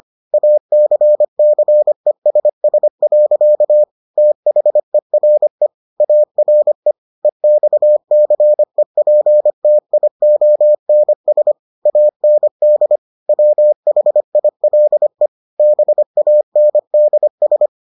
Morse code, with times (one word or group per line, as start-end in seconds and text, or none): 0.3-3.8 ACCESS.
4.2-5.7 THERE
6.0-6.9 ARE
7.2-11.5 EXCEPTIONS
11.9-13.0 AND
13.3-15.3 WHILE
15.6-17.7 BANDS